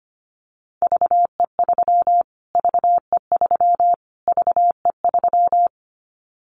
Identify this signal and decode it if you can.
Morse code, heard as a keyed tone, about 25 wpm, 700 Hz. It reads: VE3 VE3 VE3
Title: Callsign Prefixes for Top 25 DXCC - Repeated 3x - 25wpm